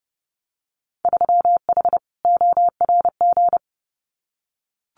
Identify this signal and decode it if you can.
Morse code, heard as a keyed tone, about 30 wpm, 700 Hz. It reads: 3H ORG
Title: Winter Field Day Exchanges Encore 30wpm